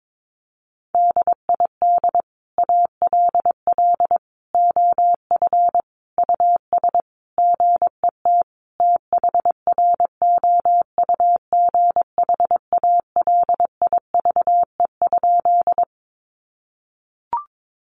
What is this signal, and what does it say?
Morse code, heard as a keyed tone, about 22 wpm, 700 Hz. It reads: DID ALL OF US GET THROUGH ALIVE?
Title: A Sound Of Thunder 22wpm